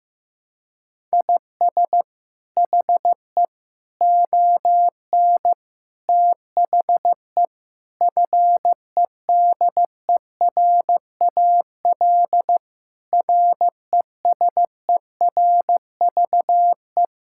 Morse code, heard as a keyed tone, about 15 wpm, 700 Hz. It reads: IS HE ON THE FEDERAL RESERVE
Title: Sentences from Top 1000 Words - Review Part 6 - 15wpm